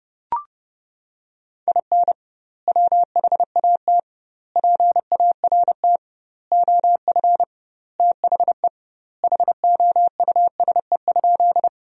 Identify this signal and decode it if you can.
Morse code, heard as a keyed tone, about 30 wpm, 700 Hz. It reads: IN WHAT PART OF THE HOUSE?
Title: 2x Word-Spacing - Sentences from Top 200 Words - 30wpm